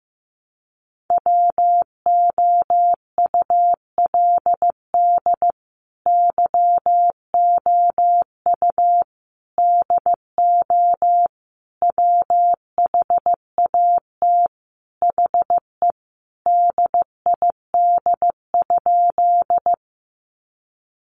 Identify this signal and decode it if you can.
Morse code, heard as a keyed tone, about 15 wpm, 700 Hz. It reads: WOULD YOU DO WHAT HE DID?
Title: Sentences from Top 100 Words 15wpm